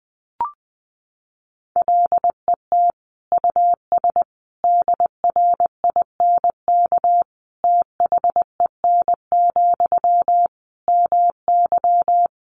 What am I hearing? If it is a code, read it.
Morse code, heard as a keyed tone, about 20 wpm, 700 Hz. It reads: LET US DRINK THEN, MY
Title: Sentences from Top 1500 Words 20wpm